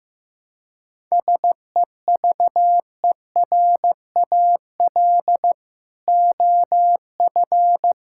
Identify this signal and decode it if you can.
Morse code, heard as a keyed tone, about 15 wpm, 700 Hz. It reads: SEVERAL OF
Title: Sentences from Top 1000 Words - Review Part 5 - 15wpm